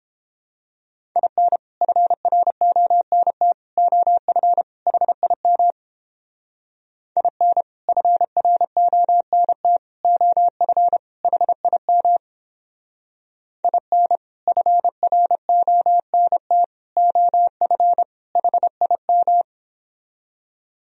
Morse code, heard as a keyed tone, about 29 wpm, 700 Hz. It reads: IN FRONT OF HIM IN FRONT OF HIM IN FRONT OF HIM